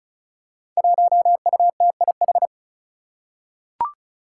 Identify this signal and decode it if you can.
Morse code, heard as a keyed tone, about 36 wpm, 700 Hz. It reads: 1UTIH
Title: Five Challenging Letter-Number Combinations 35wpm